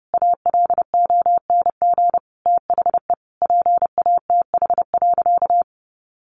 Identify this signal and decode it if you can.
Morse code, heard as a keyed tone, about 30 wpm, 700 Hz. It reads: ALONG THE PATH.
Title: A Sound Of Thunder 30wpm